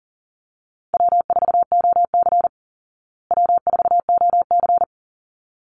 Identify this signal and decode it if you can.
Morse code, heard as a keyed tone, about 40 wpm, 700 Hz. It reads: W4OC W4OC
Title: Easy Call Signs Repeated 2x - 40wpm